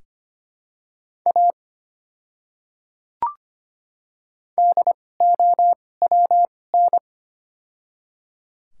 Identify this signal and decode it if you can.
Morse code, heard as a keyed tone, about 25 wpm, spaced as for 18 wpm, 700 Hz. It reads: A DOWN
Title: 2x Character Spacing - Top 200 Words - 25wpm